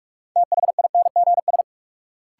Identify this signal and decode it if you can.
Morse code, heard as a keyed tone, about 46 wpm, 700 Hz. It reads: THINGS
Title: Sets of 3 Words - New Word Drill - 45wpm